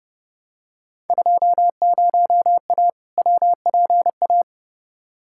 Morse code, heard as a keyed tone, about 29 wpm, 700 Hz. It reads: 20A WPA
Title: Field Day Exchanges Encore - 30wpm